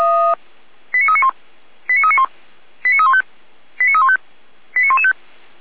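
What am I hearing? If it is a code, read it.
Five-tone selective call (ZVEI), heard as quick runs of five tones, tones of about 70 ms. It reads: 89391 89391 89315 89315 89115
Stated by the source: ZVEI five-tone selective call received on SDR by Feuerwehr Orlamünde.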